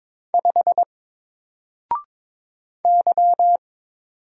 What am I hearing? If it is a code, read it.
Morse code, heard as a keyed tone, about 22 wpm, 700 Hz. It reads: 5 Y
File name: Single-Letter-Number-22wpm